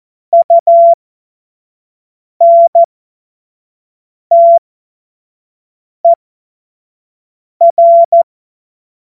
Morse code, heard as a keyed tone, about 14 wpm, spaced as for 5 wpm, 680 Hz. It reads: UNTER
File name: Morsekurs / Witze / Witz 04. Oktober 2022 25 F70